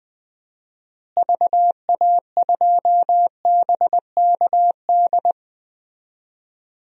Morse code, heard as a keyed tone, about 21 wpm, 700 Hz. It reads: VA2BKD
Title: DXCC-100 Call Signs - 6 Characters - Repeated 2x - 20wpm